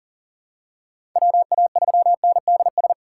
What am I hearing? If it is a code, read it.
Morse code, heard as a keyed tone, about 39 wpm, 700 Hz. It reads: WA3NDS
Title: USA Call Signs - 6 Characters - 6x Character Spacing - 40wpm